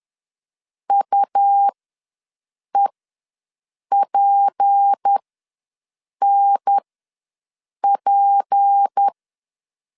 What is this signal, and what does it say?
Morse code, heard as a keyed tone, about 10 wpm, spaced as for 6 wpm, 800 Hz. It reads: UEPNP
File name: LEZIONI / 6 PKX0 / 6 P ws 10wpm 3751